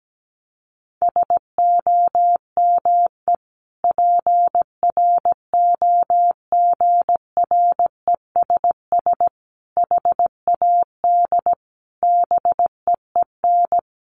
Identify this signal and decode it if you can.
Morse code, heard as a keyed tone, about 17 wpm, 700 Hz. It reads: SOME PROGRESS HAD BEEN
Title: Sentences from Top 1400 Words 17wpm